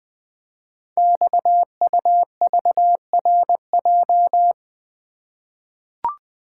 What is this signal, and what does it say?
Morse code, heard as a keyed tone, about 20 wpm, 700 Hz. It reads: XUVRJ
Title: Code Groups - 5 Letters - 20wpm